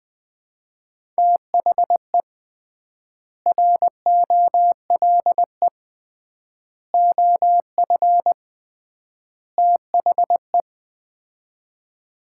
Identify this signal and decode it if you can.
Morse code, heard as a keyed tone, about 20 wpm, 700 Hz. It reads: THE ROLE OF THE
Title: 3x Word-Spacing - Sets of 4 Words - 20wpm